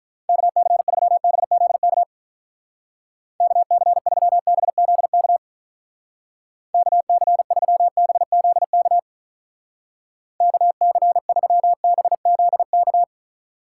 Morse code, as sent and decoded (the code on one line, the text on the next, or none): -.- -.-. ...-- -... --.. -.- / -.- -.-. ...-- -... --.. -.- / -.- -.-. ...-- -... --.. -.- / -.- -.-. ...-- -... --.. -.-
KC3BZK KC3BZK KC3BZK KC3BZK